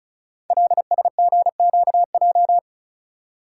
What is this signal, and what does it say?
Morse code, heard as a keyed tone, about 36 wpm, 700 Hz. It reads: LSGQJ
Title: Code Groups - 5 Letter:Numbers - 35wpm